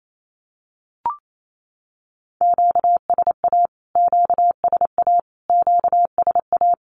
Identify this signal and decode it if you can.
Morse code, heard as a keyed tone, about 28 wpm, 700 Hz. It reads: QSA QSA QSA
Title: Q-Codes Repeated 3x 28wpm